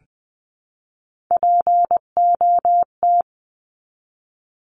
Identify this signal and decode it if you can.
Morse code, heard as a keyed tone, about 20 wpm, 700 Hz. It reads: POT
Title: 259 Lesson 42d - Words 18-20wpm